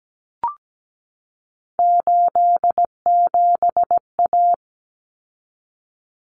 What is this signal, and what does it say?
Morse code, heard as a keyed tone, about 17 wpm, 700 Hz. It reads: 87A
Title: Common Cut Numbers - 3-Digit 17wpm